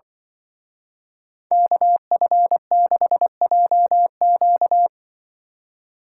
Morse code, heard as a keyed tone, about 24 wpm, 700 Hz. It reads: KF6JQ
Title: Speed Racing - USA 2x2 Call Signs 20wpm